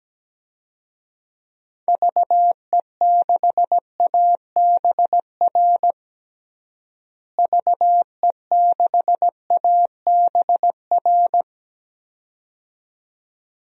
Morse code, heard as a keyed tone, about 17 wpm, 700 Hz. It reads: VE6ABR VE6ABR
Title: DXCC-100 Call Signs - 6 Characters - Repeated 2x - 17wpm